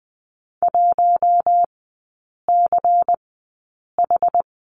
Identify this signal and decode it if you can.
Morse code, heard as a keyed tone, about 20 wpm, spaced as for 8 wpm, 700 Hz. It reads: 1CH
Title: 093 Lesson 16e - Letter c - Three Characters 20wpm